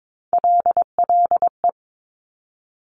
Morse code, heard as a keyed tone, about 22 wpm, 700 Hz. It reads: LLE
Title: Top Three Letter Combinations in Words 22wpm